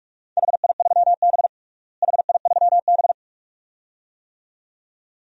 Morse code, heard as a keyed tone, about 46 wpm, 700 Hz. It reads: HI3B HI3B
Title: DXCC-100 Call Signs - 4 Characters - Repeated 2x - 45wpm